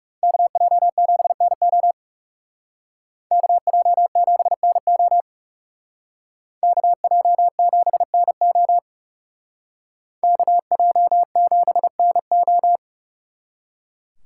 Morse code, heard as a keyed tone, about 37 wpm, 700 Hz. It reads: KJ7NO KJ7NO KJ7NO KJ7NO
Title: Speed Racing - USA 2x2 Call Signs 30wpm